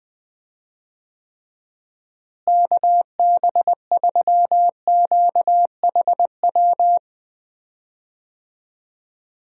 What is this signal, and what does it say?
Morse code, heard as a keyed tone, about 20 wpm, 700 Hz. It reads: KB3QHW
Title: USA Call Signs - 6 Characters - Repeated 2x - 20wpm